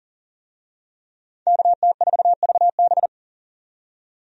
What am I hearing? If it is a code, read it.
Morse code, heard as a keyed tone, about 37 wpm, 700 Hz. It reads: KT4VB
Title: USA Call Signs - 5 Characters - 40wpm